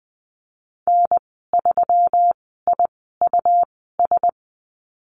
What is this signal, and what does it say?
Morse code, heard as a keyed tone, about 20 wpm, spaced as for 15 wpm, 700 Hz. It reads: N3IUS
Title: 154 Lesson 25g - Letter p - Call Signs 20wpm